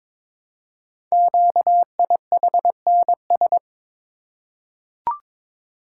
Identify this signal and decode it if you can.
Morse code, heard as a keyed tone, about 22 wpm, 700 Hz. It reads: QIHNS
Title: Code Groups - 5 Letters - 22wpm